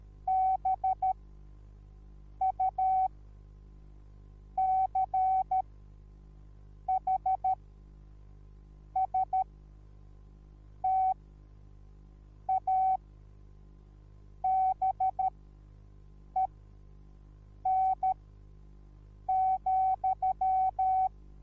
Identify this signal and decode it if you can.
Morse code, heard as a keyed tone, about 13 wpm, spaced as for 5 wpm, 760 Hz. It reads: BUCHSTABEN,
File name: Onlinelehrgang / morsen / DJ4UF Morsen21